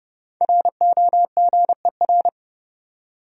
Morse code, heard as a keyed tone, about 30 wpm, 700 Hz. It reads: ROGER
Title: USA Common Names - Max 5 Letters - 30wpm